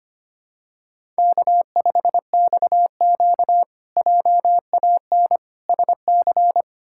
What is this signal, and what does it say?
Morse code, heard as a keyed tone, about 25 wpm, 700 Hz. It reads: K5XQ JAN SC
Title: K1USN SST QSO Practice - 25wpm